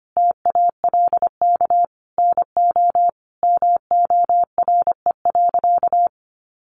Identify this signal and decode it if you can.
Morse code, heard as a keyed tone, about 25 wpm, 700 Hz. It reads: TALK NO MORE.
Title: Sentences from Top 1400 Words 25wpm